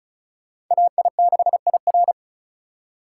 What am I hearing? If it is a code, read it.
Morse code, heard as a keyed tone, about 45 wpm, 700 Hz. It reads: AI6IR